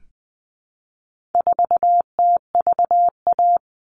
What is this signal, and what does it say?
Morse code, heard as a keyed tone, about 20 wpm, 700 Hz. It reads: 4TVA
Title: All Cut Numbers - 4-Digit 20wpm